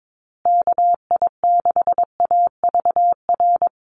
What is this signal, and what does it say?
Morse code, heard as a keyed tone, about 22 wpm, 700 Hz. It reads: KI6AVR
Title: USA Call Signs Repeated 2x 22wpm